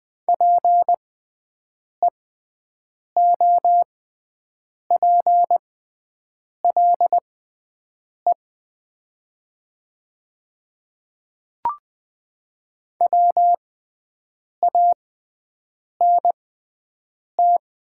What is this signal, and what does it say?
Morse code, heard as a keyed tone, about 20 wpm, spaced as for 7 wpm, 700 Hz. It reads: PEOPLE WANT
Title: 6x Character Spacing - Top 200 Words - 20wpm